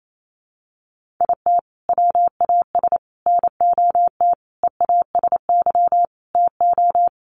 Morse code, heard as a keyed tone, about 28 wpm, 700 Hz. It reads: IT WAS NOT EASY TO
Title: Sentences from Top 1500 Words 28wpm